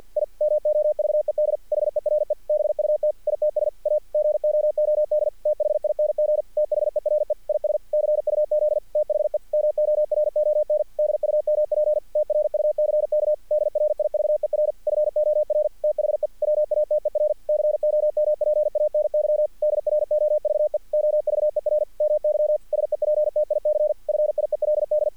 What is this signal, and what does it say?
Morse code, heard as a keyed tone, about 50 wpm, 600 Hz. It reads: I MOVED HERE BUT ITS A GOOD THING THERE IS CUZ THE MORON DUMP TRUCK DRIVER FOR THE WATER COMPANY DROVE OVER MY SEPTIC FIELD
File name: archived / CRSnet